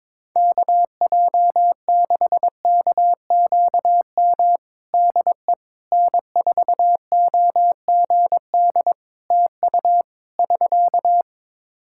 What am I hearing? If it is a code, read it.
Morse code, heard as a keyed tone, about 22 wpm, 700 Hz. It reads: KJ6KQM DE N4OGD TU <SK>
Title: Practice QSO - Exchange 4 - Ending - 22wpm